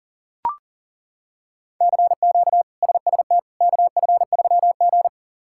Morse code, heard as a keyed tone, about 37 wpm, 700 Hz. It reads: CQ SST KF3G
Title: K1USN SST QSO Practice - 40wpm